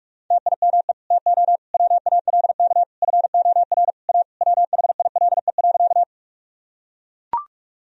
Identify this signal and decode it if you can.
Morse code, heard as a keyed tone, about 41 wpm, 700 Hz. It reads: TIME TO WALK FOR A WHILE.
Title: Sentences from Top 1000 Words - Review Part 1 - 45wpm